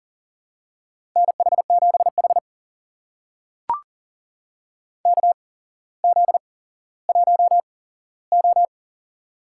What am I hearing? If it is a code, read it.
Morse code, heard as a keyed tone, about 42 wpm, spaced as for 17 wpm, 700 Hz. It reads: NH7H KZ1O
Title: USA Call Signs - 4 Characters - 8x Character Spacing - 40wpm